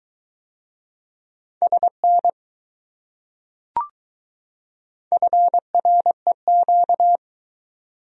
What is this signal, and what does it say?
Morse code, heard as a keyed tone, about 20 wpm, 700 Hz. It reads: SN FREQ